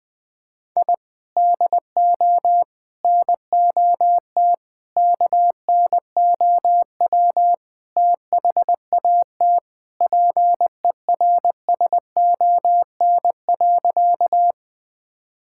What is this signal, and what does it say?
Morse code, heard as a keyed tone, about 21 wpm, 700 Hz. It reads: I DO NOT KNOW THAT PERSON.
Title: Sentences with 5-Word Phrases - Spoken Beginning - 20wpm